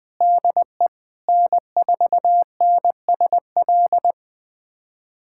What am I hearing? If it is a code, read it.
Morse code, heard as a keyed tone, about 20 wpm, 700 Hz. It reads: DE N4NSL
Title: USA Callsign DE Callsign - 5 Characters - 20wpm